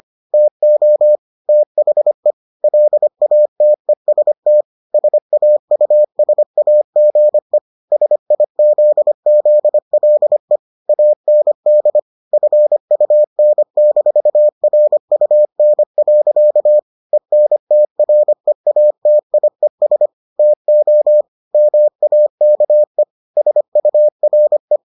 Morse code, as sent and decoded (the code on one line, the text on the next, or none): - --- / - .... / .-.. .- - . ... - / ... .- ..- ... .- --. / ... .. --.. --.. .-.. / .- -. -.. / ..-. ..- -. -....- .-. ..- -. .-.-.- / . -. - .-. .- - .. ... / - --- / -- .- -.- . / ... ..- .-. .
TO THE LATEST SAUSAGE SIZZLE AND FUN-RUN. ENTREATIES TO MAKE SURE